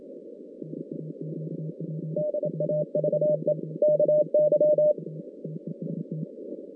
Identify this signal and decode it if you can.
Morse code, heard as a keyed tone, about 27 wpm, 600 Hz. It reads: DAVE KY
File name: 2016 JAN CW NAQP / 2016 Jan naqp k5ot - 0-9